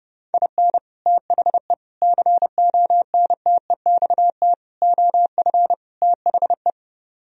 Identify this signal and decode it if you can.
Morse code, heard as a keyed tone, about 32 wpm, 700 Hz. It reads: IN THE CONTEXT OF THE